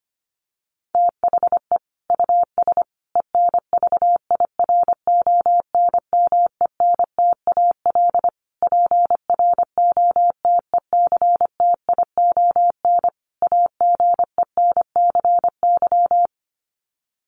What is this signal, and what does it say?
Morse code, heard as a keyed tone, about 25 wpm, 700 Hz. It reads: THE US ENVIRONMENTAL PROTECTION AGENCY
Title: Sets of 5 Words Encore2 25wpm